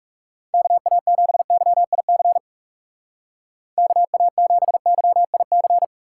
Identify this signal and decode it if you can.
Morse code, heard as a keyed tone, about 37 wpm, 700 Hz. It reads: KA7YIC KA7YIC
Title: Speed Racing - USA 2x3 Call Signs 30wpm